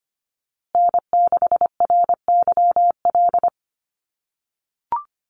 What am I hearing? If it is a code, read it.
Morse code, heard as a keyed tone, about 25 wpm, 700 Hz. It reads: N6RYL
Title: USA Call Signs Repeated 3x 25wpm